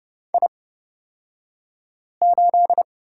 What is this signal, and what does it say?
Morse code, heard as a keyed tone, about 30 wpm, spaced as for 5 wpm, 700 Hz. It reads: I8
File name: Lesson 38d - Letter q - Two Characters 30wpm